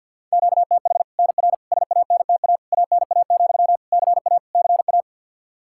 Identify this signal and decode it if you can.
Morse code, heard as a keyed tone, about 52 wpm, 700 Hz. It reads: QTH NR SANTA ANA, CA CA
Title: Practice QSO - Exchange 1 - Beginning - 50wpm